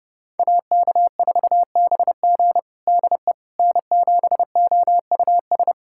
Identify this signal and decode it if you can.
Morse code, heard as a keyed tone, about 32 wpm, 700 Hz. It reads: AK4BG DE N7OUS
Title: USA Callsign DE Callsign - 5 Characters - 30wpm